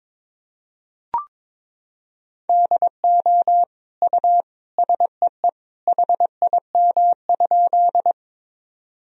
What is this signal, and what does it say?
Morse code, heard as a keyed tone, about 22 wpm, 700 Hz. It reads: DO U SEE HIM?
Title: Sentences from Top 100 Words and Abbreviations 22wpm